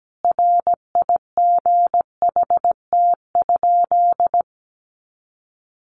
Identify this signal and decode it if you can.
Morse code, heard as a keyed tone, about 17 wpm, 700 Hz. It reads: RIGHT?
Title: Sentences from Top 200 Words 17wpm